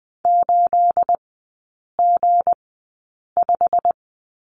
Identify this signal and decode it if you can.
Morse code, heard as a keyed tone, about 20 wpm, spaced as for 8 wpm, 690 Hz. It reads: 8G5
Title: 212 Lesson 34e - Number 8 - Three Characters 20wpm